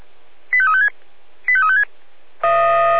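Five-tone selective call (ZVEI), heard as quick runs of five tones, tones of about 70 ms. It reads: 85356 85356
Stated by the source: Five-tone ZVEI selective call received on SDR.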